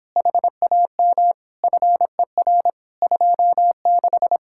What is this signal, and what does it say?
Morse code, heard as a keyed tone, about 26 wpm, 700 Hz. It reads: HAM FER 26